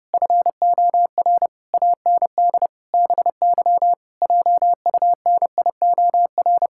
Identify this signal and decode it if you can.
Morse code, heard as a keyed tone, about 30 wpm, 700 Hz. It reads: FOR AND BY JUNIOR
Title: Pangram Challenge v2 - 30wpm